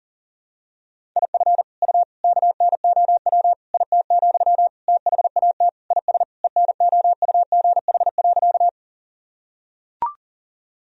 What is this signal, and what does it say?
Morse code, heard as a keyed tone, about 40 wpm, 700 Hz. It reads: IF U KNOW IT, THAT IS ENOUGH.
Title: Sentences from Top 400 Words and Abbreviations 40wpm